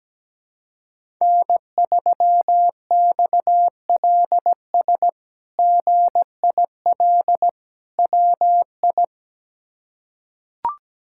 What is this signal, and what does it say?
Morse code, heard as a keyed tone, about 17 wpm, 700 Hz. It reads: N3XLS GIL WI
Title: K1USN SST QSO Practice - 17wpm